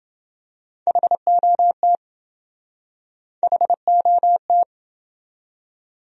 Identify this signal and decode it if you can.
Morse code, heard as a keyed tone, about 28 wpm, 700 Hz. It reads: HOT HOT